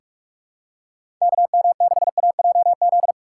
Morse code, heard as a keyed tone, about 45 wpm, 700 Hz. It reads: KM6AJZ
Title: USA Call Signs - 6 Characters - 6x Character Spacing - 45wpm